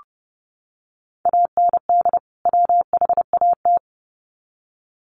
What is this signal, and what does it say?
Morse code, heard as a keyed tone, about 30 wpm, 700 Hz. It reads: AND WHAT
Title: Sets of 2 Words 30wpm